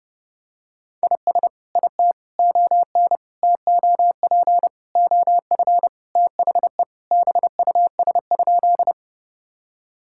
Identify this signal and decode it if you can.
Morse code, heard as a keyed tone, about 30 wpm, 700 Hz. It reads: IS IT ON TOP OF THE BUS?
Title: Sentences from Top 1000 Words - Review Part 6 - 30wpm